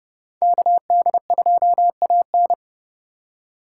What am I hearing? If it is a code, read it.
Morse code, heard as a keyed tone, about 37 wpm, 700 Hz. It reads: KD2AN
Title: Speed Racing - USA 2x2 Call Signs 30wpm